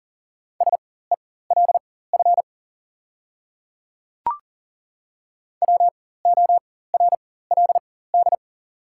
Morse code, heard as a keyed tone, about 40 wpm, spaced as for 19 wpm, 690 Hz. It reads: SELF WORLD
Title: 4x Character Spacing - Top 200 Words - 40wpm